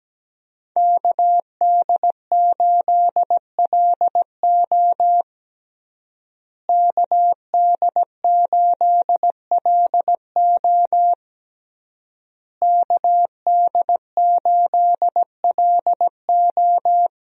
Morse code, heard as a keyed tone, about 17 wpm, 700 Hz. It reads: KD8LO KD8LO KD8LO
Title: USA Call Signs - 5 Characters - Repeated 3x - 17wpm